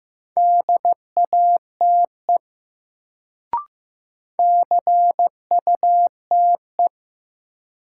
Morse code, heard as a keyed tone, about 15 wpm, 700 Hz. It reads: DATE CUTE
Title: Sentences from Top 1000 Words - Review Part 7 - New Word Drill - 15wpm